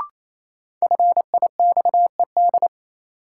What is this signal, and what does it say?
Morse code, heard as a keyed tone, about 28 wpm, 700 Hz. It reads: FIXED
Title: Sentences from Top 1100 Words - New Word Drill - 28wpm